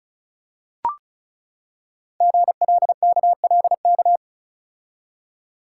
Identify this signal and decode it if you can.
Morse code, heard as a keyed tone, about 36 wpm, 700 Hz. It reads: GLKLK
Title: Code Groups - 5 Letters - 35wpm